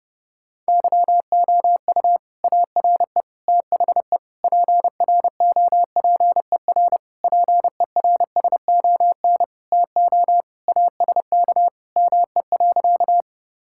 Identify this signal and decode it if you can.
Morse code, heard as a keyed tone, about 32 wpm, 700 Hz. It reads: YOU ARE THE PROPER PERSON TO ASK ME.
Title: Sentences from Top 1000 Words - Review Part 5 - 30wpm